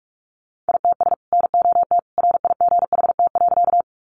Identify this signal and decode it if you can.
Morse code, heard as a keyed tone, about 44 wpm, 700 Hz. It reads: ITS NOT RIGHT.
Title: Sentences from Top 600 Words 45wpm